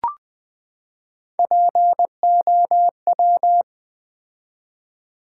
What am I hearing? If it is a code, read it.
Morse code, heard as a keyed tone, about 20 wpm, 700 Hz. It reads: POW
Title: Build A Word - 3 Letters At A Time - 8-Letter Words - 20wpm